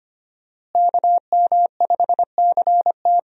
Morse code, heard as a keyed tone, about 25 wpm, 700 Hz. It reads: KM5CT
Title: USA Call Signs - 5 Characters - Repeated 2x - 25wpm